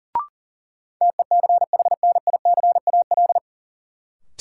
Morse code, heard as a keyed tone, about 37 wpm, 700 Hz. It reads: TECHNICAL
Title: Sentences from Top 1100 Words - New Word Drill - 40wpm